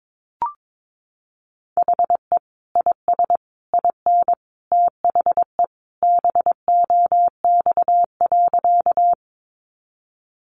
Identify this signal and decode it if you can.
Morse code, heard as a keyed tone, about 22 wpm, 700 Hz. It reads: HE IS IN THE BOX.
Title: Sentences from Top 500 Words 22wpm